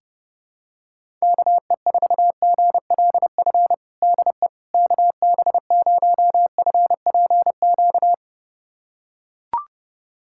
Morse code, heard as a keyed tone, about 30 wpm, 700 Hz. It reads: KE4GLF DE KB0FPQ
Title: USA Callsign DE Callsign 30wpm